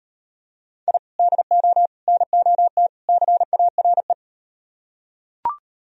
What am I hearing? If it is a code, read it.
Morse code, heard as a keyed tone, about 39 wpm, 700 Hz. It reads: I DO NOT CARE